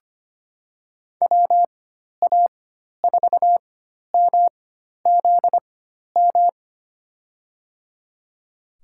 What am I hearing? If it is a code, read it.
Morse code, heard as a keyed tone, about 25 wpm, spaced as for 12 wpm, 700 Hz. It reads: WA4MZM
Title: USA Call Signs - 6 Characters - 4x Character Spacing - 25wpm